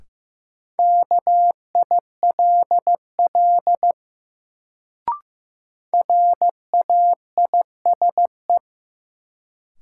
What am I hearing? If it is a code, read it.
Morse code, heard as a keyed tone, about 15 wpm, 700 Hz. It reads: KILL RAISE
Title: Sentences from Top 1400 Words - New Word Drill - 15wpm